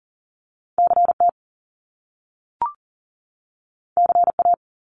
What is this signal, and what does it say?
Morse code, heard as a keyed tone, about 38 wpm, 700 Hz. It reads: CT CA
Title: US States 40wpm